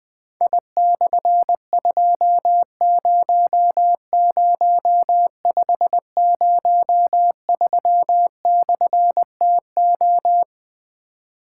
Morse code, heard as a keyed tone, about 20 wpm, 700 Hz. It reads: I/200503/TO
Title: Difficult Call Signs 20wpm